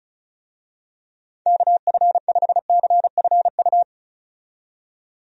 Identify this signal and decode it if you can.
Morse code, heard as a keyed tone, about 36 wpm, 700 Hz. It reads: KF5CFU
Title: USA Call Signs - 6 Characters - 35wpm